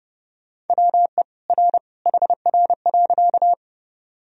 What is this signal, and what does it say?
Morse code, heard as a keyed tone, about 30 wpm, 700 Hz. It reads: WE R HR.
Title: Sentences from Top 200 Words and Abbreviations 30wpm